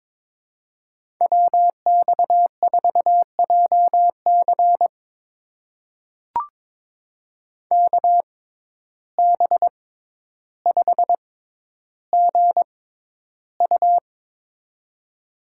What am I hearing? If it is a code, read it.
Morse code, heard as a keyed tone, about 22 wpm, spaced as for 11 wpm, 700 Hz. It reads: WX4JC KB5GU
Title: USA Call Signs - 6x Character Spacing - 22wpm